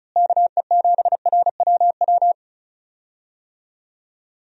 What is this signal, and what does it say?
Morse code, heard as a keyed tone, about 36 wpm, 700 Hz. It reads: KE7RWW
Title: USA Call Signs - 6 Characters - Repeated 2x - 35wpm